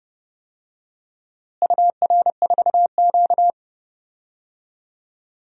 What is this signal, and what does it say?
Morse code, heard as a keyed tone, about 32 wpm, 700 Hz. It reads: UR4Q
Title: DXCC-100 Call Signs - 4 Characters - Repeated 2x - 30wpm